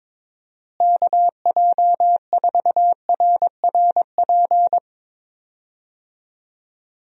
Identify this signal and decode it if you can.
Morse code, heard as a keyed tone, about 22 wpm, 700 Hz. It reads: KJ4RRP
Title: USA Call Signs - 6 Characters - Repeated 2x - 22wpm